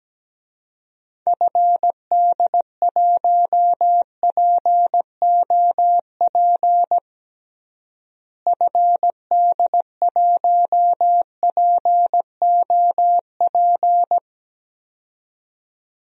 Morse code, heard as a keyed tone, about 17 wpm, 700 Hz. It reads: FD1POP FD1POP
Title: DXCC-100 Call Signs - 6 Characters - Repeated 2x - 17wpm